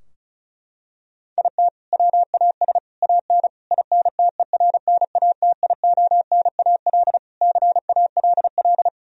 Morse code, heard as a keyed tone, about 36 wpm, 700 Hz. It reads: IT WAS AN INTERNATIONAL CALL
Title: Sentences from Top 1000 Words - Review Part 4 - 35wpm